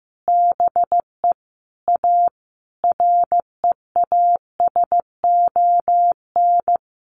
Morse code, heard as a keyed tone, about 15 wpm, 700 Hz. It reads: BE A REASON
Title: Sentences from Top 1300 Words 15wpm